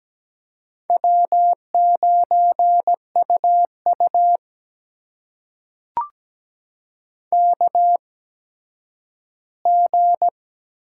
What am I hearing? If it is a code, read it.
Morse code, heard as a keyed tone, about 17 wpm, spaced as for 10 wpm, 700 Hz. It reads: W9UU KG
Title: USA Call Signs - 4 Characters - 8x Character Spacing - 17wpm